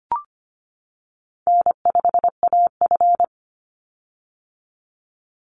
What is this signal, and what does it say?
Morse code, heard as a keyed tone, about 25 wpm, 700 Hz. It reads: N5AF
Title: Easy Call Signs - 25wpm